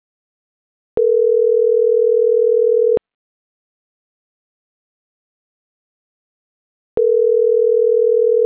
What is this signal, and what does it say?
Ringback tone (North America), 440 + 480 Hz.
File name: CENTRAL MABE / out miriam.soledad